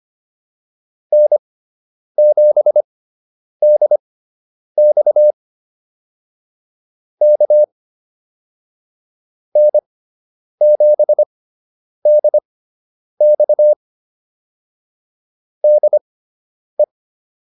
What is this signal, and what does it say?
Morse code, heard as a keyed tone, about 25 wpm, spaced as for 9 wpm, 600 Hz. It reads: N7DX K N7DX DE